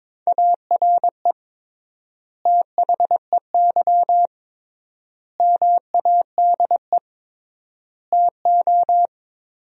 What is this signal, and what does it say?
Morse code, heard as a keyed tone, about 22 wpm, 700 Hz. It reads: ARE THEY MADE TO
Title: 3x Word-Spacing - Sentences from Top 400 Words - 22wpm